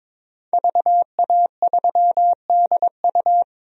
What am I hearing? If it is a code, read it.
Morse code, heard as a keyed tone, about 22 wpm, 700 Hz. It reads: VA3DU
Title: DXCC-100 Call Signs - 5 Characters - 22wpm